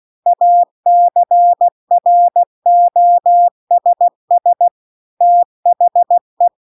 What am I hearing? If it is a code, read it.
Morse code, heard as a keyed tone, about 16 wpm, 700 Hz. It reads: ACROSS THE